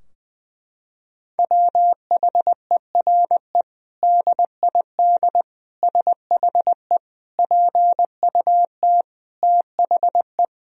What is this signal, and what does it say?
Morse code, heard as a keyed tone, about 20 wpm, 700 Hz. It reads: WHERE DID SHE PUT THE
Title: Sentences from Top 1000 Words - Review Part 3 - 20wpm